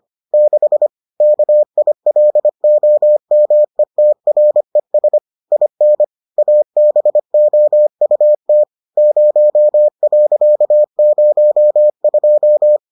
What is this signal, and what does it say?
Morse code, heard as a keyed tone, about 25 wpm, 600 Hz. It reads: 6 KILOMETRES IN ABOUT 0.02